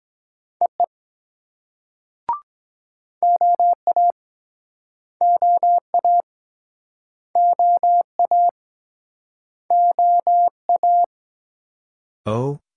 Morse code, heard as a keyed tone, about 21 wpm, 700 Hz. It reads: EE OA OA OA OA
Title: Speed-Racing Multi-letter Phonemes - 17wpm